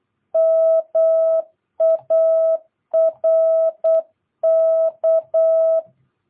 Morse code, heard as a keyed tone, about 8 wpm, 650 Hz. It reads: MARK
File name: jh8cbh / 2022cw